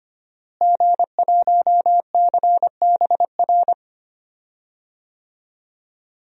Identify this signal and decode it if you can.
Morse code, heard as a keyed tone, about 25 wpm, 700 Hz. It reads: G1CBR
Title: DXCC-100 Call Signs - 5 Characters - Repeated 3x - 25wpm